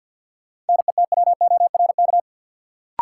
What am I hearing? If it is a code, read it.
Morse code, heard as a keyed tone, about 53 wpm, 700 Hz. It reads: NETWORK
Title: Sentences from Top 1000 Words - Review Part 7 - New Word Drill - 50wpm